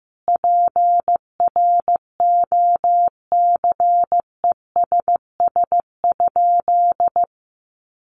Morse code, heard as a keyed tone, about 15 wpm, 700 Hz. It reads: PROCESS?
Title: Sentences from Top 1400 Words 15wpm